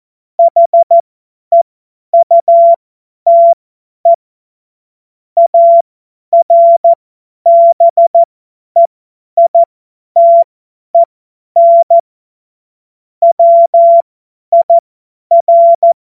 Morse code, heard as a keyed tone, about 14 wpm, spaced as for 10 wpm, 680 Hz. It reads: HEUTE ARBEITEN WIR
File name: Morsekurs / Witze / Witz 19 Dezember 2023 50 F70